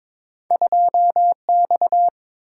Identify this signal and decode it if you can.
Morse code, heard as a keyed tone, about 22 wpm, 700 Hz. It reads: 2X
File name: practice / 22wpm / Two Letter-Number Combinations - 22wpm